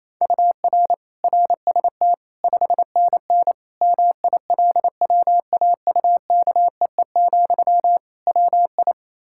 Morse code, heard as a keyed tone, about 28 wpm, 700 Hz. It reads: UR RST 5NN MILWAUKEE, WI
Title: K1USN SST QSO Practice 28wpm